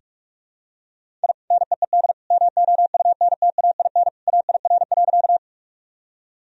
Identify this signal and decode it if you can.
Morse code, heard as a keyed tone, about 46 wpm, 700 Hz. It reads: I NEED MOUNTAIN AIR.
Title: Sentences from Top 1000 Words - Review Part 1 - 45wpm